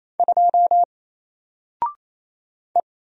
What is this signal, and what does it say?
Morse code, heard as a keyed tone, about 28 wpm, 700 Hz. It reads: 2 E
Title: Single Letter-Number Alt 28wpm